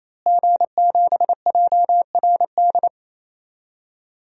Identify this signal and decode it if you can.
Morse code, heard as a keyed tone, about 28 wpm, 700 Hz. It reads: G7JRD
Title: DXCC-100 Call Signs - 5 Characters - Repeated 3x - 28wpm